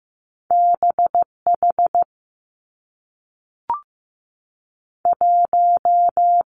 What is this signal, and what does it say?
Morse code, heard as a keyed tone, about 15 wpm, 700 Hz. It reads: BH 1
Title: Two Challenging Letter-Number Combinations 15wpm